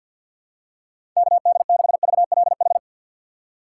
Morse code, heard as a keyed tone, about 52 wpm, 690 Hz. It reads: KD6VLH
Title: USA Call Signs - 6 Characters - 3x Character Spacing - 50wpm